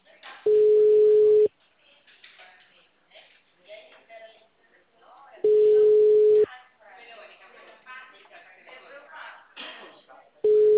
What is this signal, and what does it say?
Ringback tone (Europe, 425 Hz), with speech.
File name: out- -2007- - - .5202